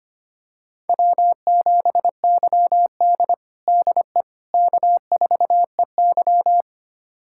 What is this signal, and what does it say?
Morse code, heard as a keyed tone, about 25 wpm, 700 Hz. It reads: W7YD DE K4EY
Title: USA Callsign DE Callsign - 4 Characters - 25wpm